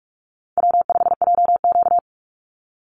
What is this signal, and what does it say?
Morse code, heard as a keyed tone, about 44 wpm, 700 Hz. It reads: W5JQ
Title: Call Signs Repeated 2x - 45wpm